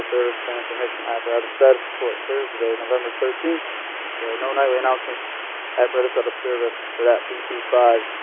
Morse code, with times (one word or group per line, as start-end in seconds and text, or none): none